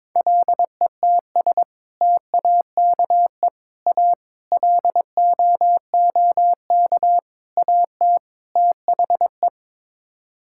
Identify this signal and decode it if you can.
Morse code, heard as a keyed tone, about 22 wpm, 700 Hz. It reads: LETS TAKE A LOOK AT THE
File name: practice / 22wpm / Sets of 6 Words - 22wpm